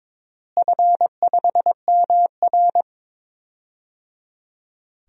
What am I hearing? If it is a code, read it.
Morse code, heard as a keyed tone, about 22 wpm, 700 Hz. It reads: F5MR
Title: DXCC-100 Call Signs - 4 Characters - Repeated 3x - 22wpm